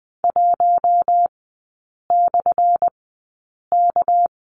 Morse code, heard as a keyed tone, about 20 wpm, spaced as for 8 wpm, 700 Hz. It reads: 1/K
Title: 198 Lesson 32e - Letter k - Three Characters 20wpm